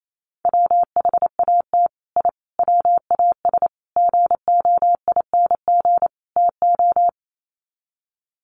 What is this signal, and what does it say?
Morse code, heard as a keyed tone, about 28 wpm, 700 Hz. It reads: WHAT I WAS GOING TO
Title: Sets of 5 Words Encore2 28wpm